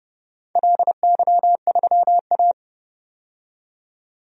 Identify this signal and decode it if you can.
Morse code, heard as a keyed tone, about 31 wpm, 700 Hz. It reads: LY3A
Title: DXCC-100 Call Signs - 4 Characters - 30wpm